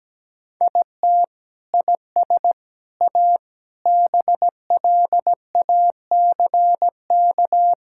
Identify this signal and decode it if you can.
Morse code, heard as a keyed tone, about 17 wpm, 700 Hz. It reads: IT IS A BLACK
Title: Sentences from Top 1000 Words - Review Part 6 - 17wpm